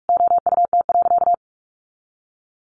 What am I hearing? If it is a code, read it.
Morse code, heard as a keyed tone, about 44 wpm, 700 Hz. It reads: OUT.
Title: Sentences from Top 500 Words 45wpm